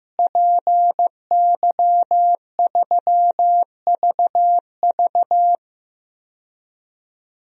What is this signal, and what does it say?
Morse code, heard as a keyed tone, about 15 wpm, 700 Hz. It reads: PY3VV
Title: DXCC-100 Call Signs - 5 Characters - Repeated 2x - 15wpm